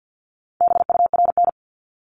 Morse code, heard as a keyed tone, about 50 wpm, 700 Hz. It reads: BURN
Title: Multi-letter Phonemes 50wpm